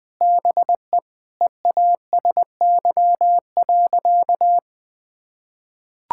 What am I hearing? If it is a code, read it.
Morse code, heard as a keyed tone, about 20 wpm, 700 Hz. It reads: BE EASY.
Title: Sentences with 4-Word Phrases - 20wpm